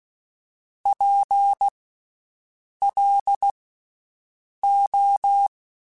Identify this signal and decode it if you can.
Morse code, heard as a keyed tone, about 16 wpm, spaced as for 6 wpm, 800 Hz. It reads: PLO
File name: Special INGs / Special INGs-mp3s / VerySlowToFaster Ing @15